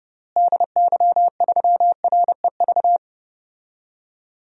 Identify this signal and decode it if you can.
Morse code, heard as a keyed tone, about 30 wpm, 700 Hz. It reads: DY3REV
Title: DXCC-100 Call Signs - 6 Characters - Repeated 3x - 30wpm